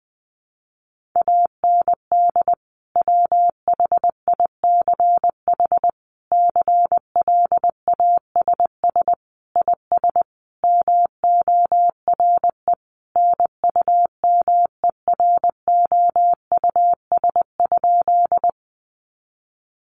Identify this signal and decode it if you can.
Morse code, heard as a keyed tone, about 20 wpm, 700 Hz. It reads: AND WHICH CLASS IS MORE NUMEROUS?
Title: Sentences from Top 1400 Words 20wpm